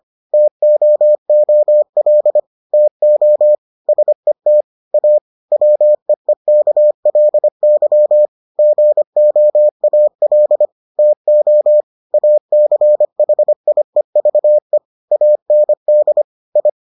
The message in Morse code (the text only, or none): TOOL TO SET A WEEKLY GOAL TO ACHIEVE AND I